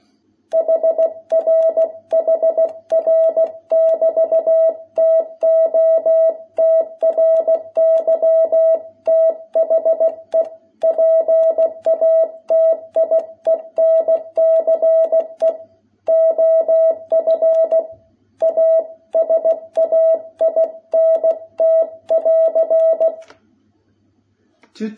Morse code, heard as a keyed tone, about 16 wpm, 660 Hz. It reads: HR HR<BT>TO TRY THE PATIENCE OF A SAINT<AR>
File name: jh8cbh / 2021cw